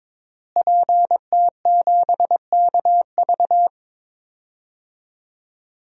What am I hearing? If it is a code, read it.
Morse code, heard as a keyed tone, about 22 wpm, 700 Hz. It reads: PT7KV
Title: DXCC-100 Call Signs - 5 Characters - Repeated 3x - 22wpm